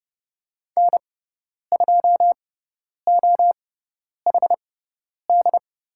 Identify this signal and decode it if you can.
Morse code, heard as a keyed tone, about 30 wpm, spaced as for 10 wpm, 700 Hz. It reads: N2OHD
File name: Lesson 16g - Letter c - Call Signs 30wpm